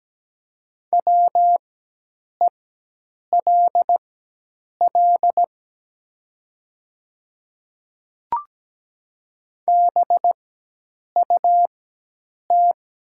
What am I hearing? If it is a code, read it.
Morse code, heard as a keyed tone, about 17 wpm, spaced as for 8 wpm, 700 Hz. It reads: WELL BUT
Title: 4x Character Spacing - Top 200 Words - 17wpm